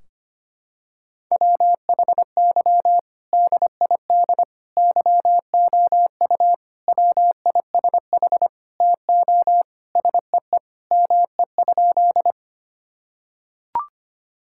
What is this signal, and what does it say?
Morse code, heard as a keyed tone, about 25 wpm, 700 Hz. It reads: WHY DID YOU WISH TO SEE ME?
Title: Sentences from Top 1600 Words 25wpm